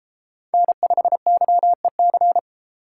Morse code, heard as a keyed tone, about 32 wpm, 700 Hz. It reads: N5YEC